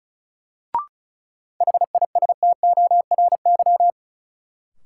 Morse code, heard as a keyed tone, about 36 wpm, 700 Hz. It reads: HISTORY
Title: Sets of 3 Words - New Word Drill - 35wpm